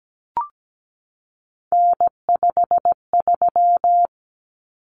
Morse code, heard as a keyed tone, about 17 wpm, 700 Hz. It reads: N53
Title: Common Cut Numbers - 3-Digit 17wpm